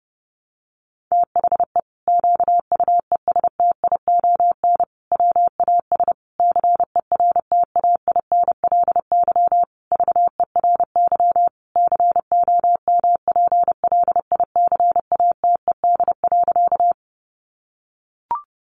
Morse code, heard as a keyed tone, about 30 wpm, 700 Hz. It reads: THE QUESTION WAS CERTAINLY VERY COMPLICATED.
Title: Sentences from Top 1200 Words 30wpm